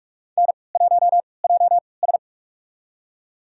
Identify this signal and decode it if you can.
Morse code, heard as a keyed tone, about 41 wpm, spaced as for 25 wpm, 700 Hz. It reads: N1JS
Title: USA Call Signs - 3x Character Spacing - 45wpm